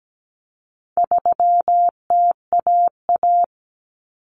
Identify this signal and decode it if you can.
Morse code, heard as a keyed tone, about 17 wpm, 700 Hz. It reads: 3TAA
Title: Common Cut Numbers - 4-Digit 17wpm